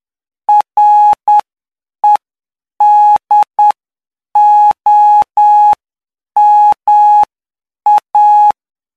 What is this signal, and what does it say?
Morse code, heard as a keyed tone, about 9 wpm, spaced as for 7 wpm, 840 Hz. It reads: REDOMA
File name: cw / py2lcd / 08-PY2LCD-Li��o 17